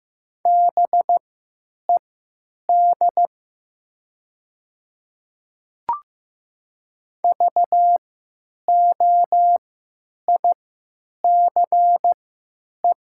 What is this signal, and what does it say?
Morse code, heard as a keyed tone, about 15 wpm, spaced as for 9 wpm, 700 Hz. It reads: BED VOICE
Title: 3x Character Spacing - Top 500 Words - 15wpm